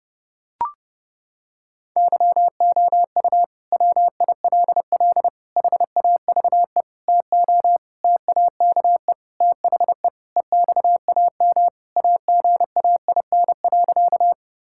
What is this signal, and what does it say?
Morse code, heard as a keyed tone, about 30 wpm, 700 Hz. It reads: YOU WILL HAVE TO TAKE THE EXAM AGAIN.
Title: Sentences from Top 1000 Words - Review Part 5 - 30wpm